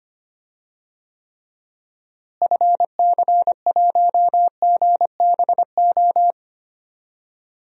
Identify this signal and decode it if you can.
Morse code, heard as a keyed tone, about 25 wpm, 700 Hz. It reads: FC1GBO
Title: DXCC-100 Call Signs - 6 Characters - Repeated 2x - 25wpm